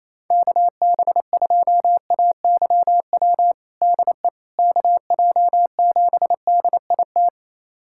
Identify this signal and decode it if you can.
Morse code, heard as a keyed tone, about 28 wpm, 700 Hz. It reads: KB2AYW DE KJ7DIT
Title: USA Callsign DE Callsign - 6 Characters - 28wpm